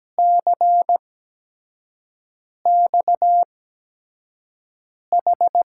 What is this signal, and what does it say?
Morse code, heard as a keyed tone, about 17 wpm, spaced as for 5 wpm, 700 Hz. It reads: CXH
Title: USA Call Signs - 8x Character Spacing - 17wpm